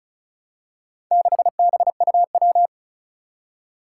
Morse code, heard as a keyed tone, about 36 wpm, 700 Hz. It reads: 6BUW
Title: Four Challenging Letter-Number Combinations 35wpm